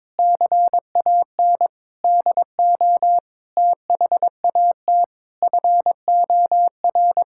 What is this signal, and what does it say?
Morse code, heard as a keyed tone, about 21 wpm, 700 Hz. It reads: CAN DO THAT FOR
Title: Sentences from Top 1000 Words - Review Part 6 - 22wpm